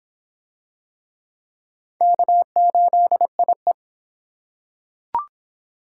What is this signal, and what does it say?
Morse code, heard as a keyed tone, about 26 wpm, 700 Hz. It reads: K8IE